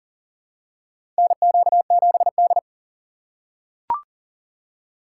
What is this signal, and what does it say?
Morse code, heard as a keyed tone, about 40 wpm, 700 Hz. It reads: NQ7D
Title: USA Call Signs - 4 Characters - 4x Character Spacing - 40wpm